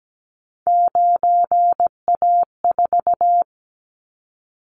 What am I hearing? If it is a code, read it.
Morse code, heard as a keyed tone, about 17 wpm, 700 Hz. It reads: 9A4
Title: Callsign Prefixes for Top 100 DXCC - 17wpm